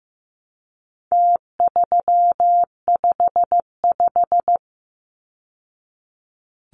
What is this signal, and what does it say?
Morse code, heard as a keyed tone, about 15 wpm, 700 Hz. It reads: T355
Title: Numbers - Contest Style - 4-Digit 15wpm